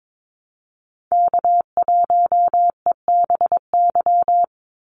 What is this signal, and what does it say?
Morse code, heard as a keyed tone, about 22 wpm, 700 Hz. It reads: K1EBY
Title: Call Signs Repeated 2x - 22wpm